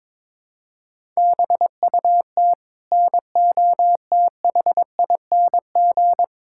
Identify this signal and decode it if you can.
Morse code, heard as a keyed tone, about 22 wpm, 700 Hz. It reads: BUT NOTHING
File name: practice / 22wpm / Sets of 8 Words - 22wpm